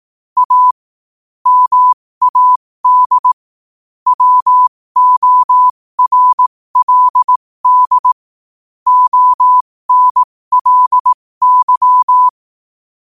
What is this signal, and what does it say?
Morse code, heard as a keyed tone, about 18 wpm, spaced as for 15 wpm, 1000 Hz. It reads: A MAD WORLD ONLY
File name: qotdfiles / 15wpm